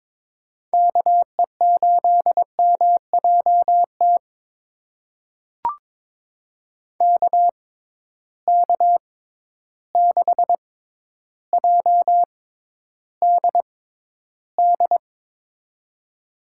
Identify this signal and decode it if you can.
Morse code, heard as a keyed tone, about 23 wpm, spaced as for 11 wpm, 700 Hz. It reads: KE8MJT KK6JDD
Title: USA Call Signs - 6 Characters - 6x Character Spacing - 22wpm